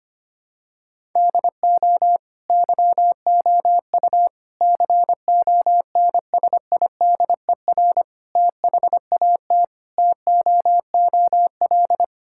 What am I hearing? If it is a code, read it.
Morse code, heard as a keyed tone, about 25 wpm, 700 Hz. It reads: DO YOU CONSIDER THAT TOOL
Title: Sentences from Top 1000 Words - Review Part 6 - 25wpm